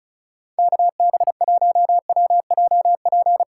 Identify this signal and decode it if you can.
Morse code, heard as a keyed tone, about 36 wpm, 700 Hz. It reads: KB1WJP
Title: USA Call Signs - 6 Characters - 2x Character Spacing - 35wpm